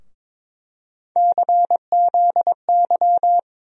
Morse code, heard as a keyed tone, about 22 wpm, 700 Hz. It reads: CZY
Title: USA Call Sign Postfixes - 3 Letters 22wpm